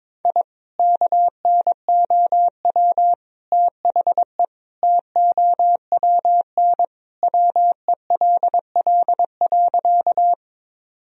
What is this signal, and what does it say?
Morse code, heard as a keyed tone, about 21 wpm, 700 Hz. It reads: I KNOW THE TOWN WELL.
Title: Sentences from Top 1000 Words - Review Part 2 - 22wpm